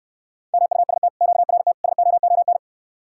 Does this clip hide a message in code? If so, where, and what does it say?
Yes: Morse code from 0.5 to 2.6 s, heard as a keyed tone, about 69 wpm, 700 Hz. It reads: MUST QRT SOON